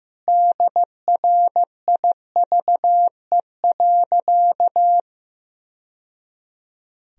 Morse code, heard as a keyed tone, about 15 wpm, 700 Hz. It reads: DRIVE.
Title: Sentences from Top 1000 Words - Review Part 2 - 15wpm